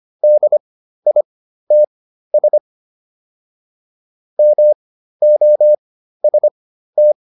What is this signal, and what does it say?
Morse code, heard as a keyed tone, about 25 wpm, spaced as for 13 wpm, 600 Hz. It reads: DITS MOST